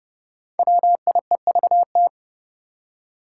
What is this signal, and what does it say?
Morse code, heard as a keyed tone, about 30 wpm, 700 Hz. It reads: WIEVT
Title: Five Challenging Letter-Number Combinations 30wpm